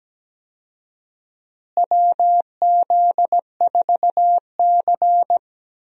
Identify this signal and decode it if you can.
Morse code, heard as a keyed tone, about 17 wpm, 700 Hz. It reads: WZ4C
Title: USA Call Signs - 4 Characters - Repeated 2x - 17wpm